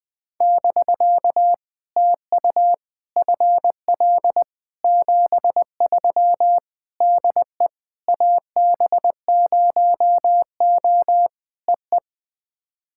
Morse code, heard as a keyed tone, about 20 wpm, 700 Hz. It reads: <BK> TU FL 73 DE AB0O EE
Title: POTA QSO Practice - 20wpm